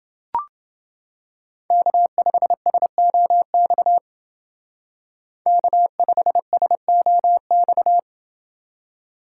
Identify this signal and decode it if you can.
Morse code, heard as a keyed tone, about 28 wpm, 700 Hz. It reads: K5SOX K5SOX